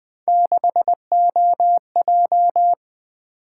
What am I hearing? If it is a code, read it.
Morse code, heard as a keyed tone, about 20 wpm, 700 Hz. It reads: 6OJ
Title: Three Letter-Number Combinations - 20wpm